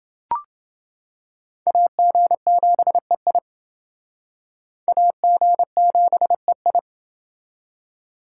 Morse code, heard as a keyed tone, about 29 wpm, 700 Hz. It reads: AG7EI AG7EI